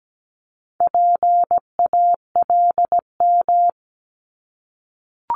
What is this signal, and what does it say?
Morse code, heard as a keyed tone, about 17 wpm, 700 Hz. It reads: PALM
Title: Multi-letter Phonemes 17wpm